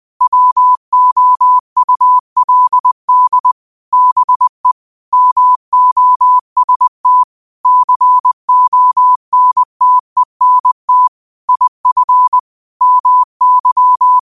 Morse code, heard as a keyed tone, about 20 wpm, 1000 Hz. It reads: WOULD BE MOST CONTENT IF MY